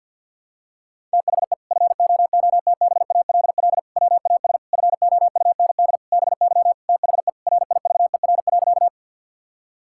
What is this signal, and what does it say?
Morse code, heard as a keyed tone, about 51 wpm, 700 Hz. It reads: THE FOOTBALL WAS FOUND BY THE RIVER.
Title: Sentences from Top 1600 Words 50wpm